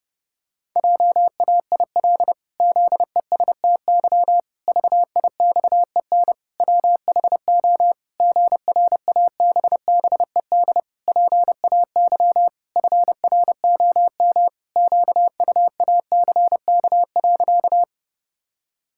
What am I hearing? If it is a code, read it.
Morse code, heard as a keyed tone, about 30 wpm, 700 Hz. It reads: JAIL ZESTY VIXEN WHO GRABBED PAY FROM QUACK.
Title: Pangram Challenge v2 - 30wpm